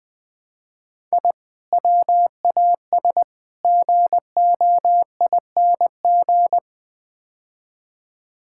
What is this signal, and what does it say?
Morse code, heard as a keyed tone, about 20 wpm, 700 Hz. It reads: I WAS GOING
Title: Build A Phrase - 5-Word Phrases - 20wpm